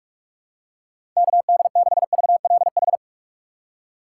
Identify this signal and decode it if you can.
Morse code, heard as a keyed tone, about 46 wpm, 700 Hz. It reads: KD6VLH
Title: USA Call Signs - 6 Characters - 6x Character Spacing - 45wpm